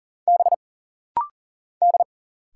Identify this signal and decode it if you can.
Morse code, heard as a keyed tone, about 41 wpm, 700 Hz. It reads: B D
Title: Single Letters - Rapid-Fire 40wpm